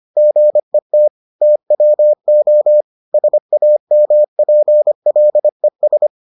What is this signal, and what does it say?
Morse code, heard as a keyed tone, about 25 wpm, 600 Hz. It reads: GET TWO SAMPLES